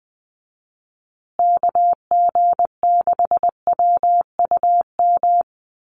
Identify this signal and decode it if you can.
Morse code, heard as a keyed tone, about 20 wpm, 700 Hz. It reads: KG6WUM
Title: USA Call Signs Repeated 3x 20wpm